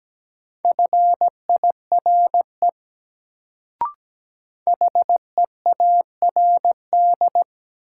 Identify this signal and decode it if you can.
Morse code, heard as a keyed tone, about 17 wpm, 700 Hz. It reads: FIRE HEARD
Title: Sentences from Top 400 Words - New Word Drill - 17wpm